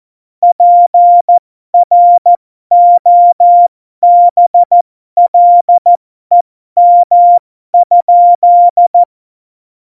Morse code, heard as a keyed tone, about 14 wpm, spaced as for 12 wpm, 700 Hz. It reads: PROBLEM?
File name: Morsekurs / Witze